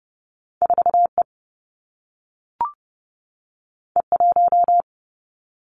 Morse code, heard as a keyed tone, about 30 wpm, 700 Hz. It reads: 4E E1
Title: Two Challenging Letter-Number Combinations 30wpm